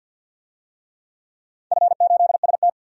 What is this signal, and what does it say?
Morse code, heard as a keyed tone, about 47 wpm, 700 Hz. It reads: F8ST
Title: DXCC-100 Call Signs - 4 Characters - 50wpm